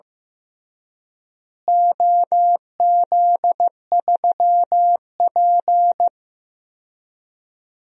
Morse code, heard as a keyed tone, about 15 wpm, 700 Hz. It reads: OZ3P
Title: DXCC-100 Call Signs - 4 Characters - Repeated 3x - 15wpm